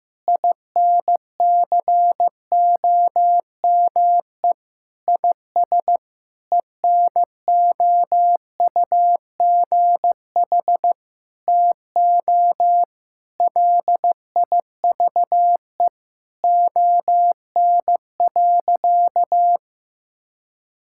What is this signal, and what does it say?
Morse code, heard as a keyed tone, about 15 wpm, 700 Hz. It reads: INCOME IS ENOUGH TO LIVE ON.
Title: Sentences from Top 1000 Words - Review Part 6 - 15wpm